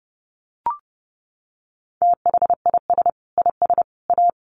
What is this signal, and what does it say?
Morse code, heard as a keyed tone, about 28 wpm, 700 Hz. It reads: THIS IS A
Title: Sentences from Top 1400 Words 30wpm